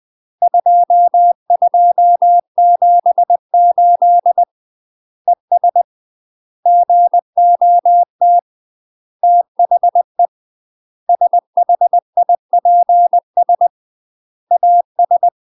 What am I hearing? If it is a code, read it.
Morse code, heard as a keyed tone, about 20 wpm, 700 Hz. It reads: 2278 ES GOT THE SHIPS AS
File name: 20WPM